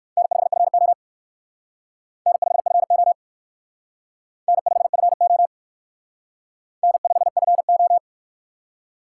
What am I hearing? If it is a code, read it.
Morse code, heard as a keyed tone, about 56 wpm, 700 Hz. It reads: N5FO N5FO N5FO N5FO
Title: Speed Racing - USA Call Signs 45wpm